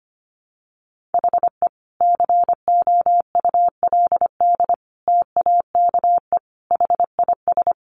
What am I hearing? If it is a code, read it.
Morse code, heard as a keyed tone, about 25 wpm, 700 Hz. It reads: HE COULD TAKE HIS
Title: Sentences from Top 900 Words 25wpm